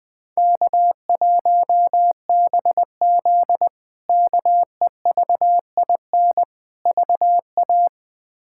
Morse code, heard as a keyed tone, about 20 wpm, 700 Hz. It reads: K1BZ KEVIN VA
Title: K1USN SST QSO Practice - 20wpm